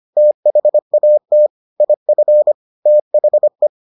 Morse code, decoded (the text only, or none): THAT IF THE